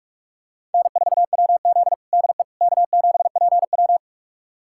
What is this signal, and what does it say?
Morse code, heard as a keyed tone, about 41 wpm, 700 Hz. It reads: N4WZ DE K7PW
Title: USA Callsign DE Callsign - 4 Characters - 45wpm